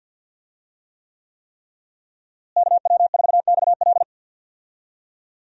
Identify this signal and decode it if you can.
Morse code, heard as a keyed tone, about 47 wpm, 700 Hz. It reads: KW4XL
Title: USA Call Signs - 5 Characters - Repeated 3x - 50wpm